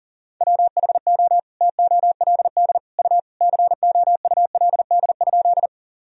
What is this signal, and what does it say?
Morse code, heard as a keyed tone, about 40 wpm, 700 Hz. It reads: WHO TOLD U COULD?
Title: Sentences from Top 400 Words and Abbreviations 40wpm